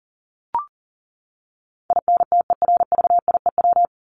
Morse code, heard as a keyed tone, about 40 wpm, 700 Hz. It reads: INTERVIEW
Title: Top 1501-1750 Words 40wpm